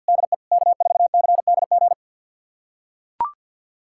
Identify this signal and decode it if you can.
Morse code, heard as a keyed tone, about 45 wpm, 700 Hz. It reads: DE K4CDG
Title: USA Callsign DE Callsign 50wpm